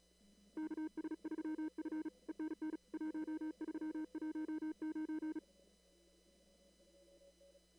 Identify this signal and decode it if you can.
Morse code, heard as a keyed tone, about 35 wpm, 310 Hz. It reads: KS3F ENN 1319
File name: CW / kt5j wpx cw-op-k5pi